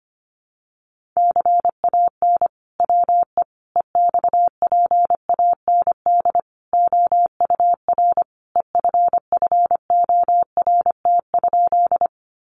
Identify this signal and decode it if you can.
Morse code, heard as a keyed tone, about 25 wpm, 700 Hz. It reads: CAN WE EXPAND OUR EFFORT?
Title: Sentences from Top 1400 Words 25wpm